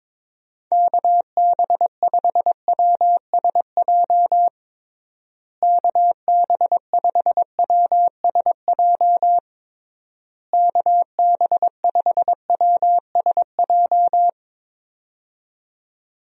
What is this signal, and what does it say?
Morse code, heard as a keyed tone, about 22 wpm, 700 Hz. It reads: KB5WSJ KB5WSJ KB5WSJ
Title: USA Call Signs - 6 Characters - Repeated 3x - 22wpm